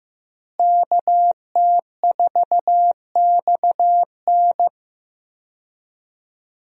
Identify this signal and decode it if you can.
Morse code, heard as a keyed tone, about 15 wpm, 700 Hz. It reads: KT4XN
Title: USA Call Signs - 5 Characters - Repeated 3x - 15wpm